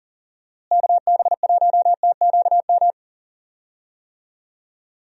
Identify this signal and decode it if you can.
Morse code, heard as a keyed tone, about 39 wpm, 700 Hz. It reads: KB1TQM
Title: USA Call Signs - 6 Characters - Repeated 3x - 40wpm